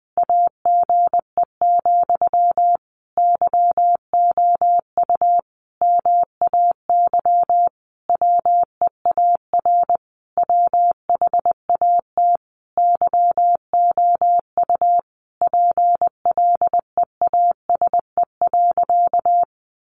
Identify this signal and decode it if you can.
Morse code, heard as a keyed tone, about 20 wpm, 700 Hz. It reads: AGE, YOU MAY WEAR WHAT YOU PLEASE.
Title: Sentences from Top 1500 Words 20wpm